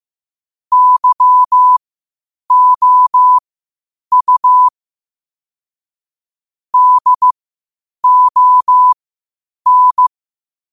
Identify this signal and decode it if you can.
Morse code, heard as a keyed tone, about 15 wpm, spaced as for 8 wpm, 1000 Hz. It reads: YOU DON
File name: qotdfiles / 8wpm